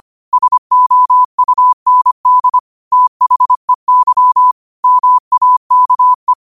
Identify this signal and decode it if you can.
Morse code, heard as a keyed tone, about 25 wpm, 1000 Hz. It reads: SOUND THEY MAKE
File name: qotdfiles / 25wpm